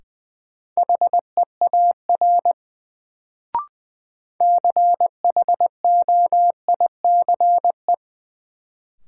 Morse code, heard as a keyed tone, about 20 wpm, 700 Hz. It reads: HEAR CHOICE
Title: Sets of 5 Words - New Word Drill - 20wpm